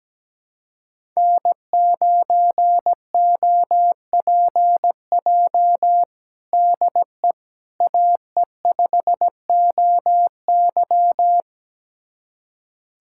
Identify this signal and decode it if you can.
Morse code, heard as a keyed tone, about 17 wpm, 700 Hz. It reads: N9OPJ DE AE5OY
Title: USA Callsign DE Callsign - 5 Characters - 17wpm